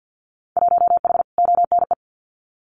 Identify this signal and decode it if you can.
Morse code, heard as a keyed tone, about 50 wpm, 700 Hz. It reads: 1H ONE
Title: Winter Field Day Exchanges Encore 50wpm